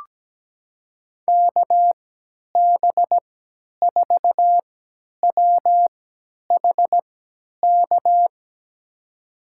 Morse code, heard as a keyed tone, about 17 wpm, spaced as for 10 wpm, 700 Hz. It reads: KB4WHK
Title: USA Call Signs - 6 Characters - 3x Character Spacing - 17wpm